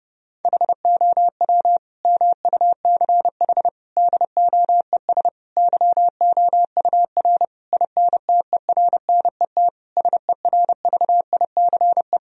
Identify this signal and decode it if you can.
Morse code, heard as a keyed tone, about 30 wpm, 700 Hz. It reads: HOW MUCH DOES YOUR INTERNET SERVICE
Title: Sentences from Top 1000 Words - Review Part 4 - 30wpm